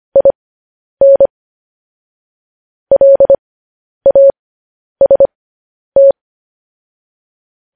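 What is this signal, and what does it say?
Morse code, heard as a keyed tone, about 25 wpm, spaced as for 10 wpm, 550 Hz. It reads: IN LAST
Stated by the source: Morse code practice sending news headlines at 10 wpm.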